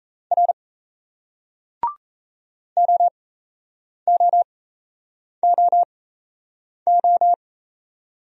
Morse code, heard as a keyed tone, about 33 wpm, spaced as for 8 wpm, 700 Hz. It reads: R OOOO
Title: Speed-Racing Single Letters 28wpm